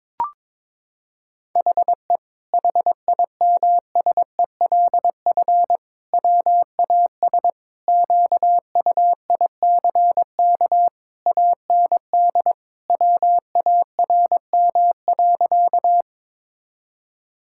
Morse code, heard as a keyed tone, about 22 wpm, 700 Hz. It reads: HE HIMSELF WAS QUICK AND WARM.
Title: Sentences from Top 1600 Words 22wpm